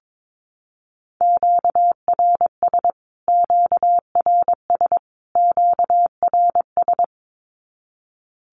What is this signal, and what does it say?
Morse code, heard as a keyed tone, about 22 wpm, 700 Hz. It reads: QRS QRS QRS
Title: Q-Codes Repeated 3x 22wpm